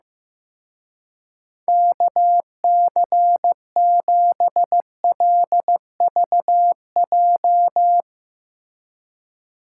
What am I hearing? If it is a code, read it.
Morse code, heard as a keyed tone, about 15 wpm, 700 Hz. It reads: KC7LVJ
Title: Speed Racing - USA 2x3 Call Signs 15wpm